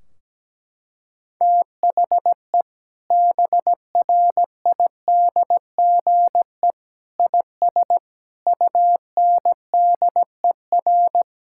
Morse code, heard as a keyed tone, about 17 wpm, 700 Hz. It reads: THE BRIDGE IS UNDER
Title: Sentences from Top 1600 Words 17wpm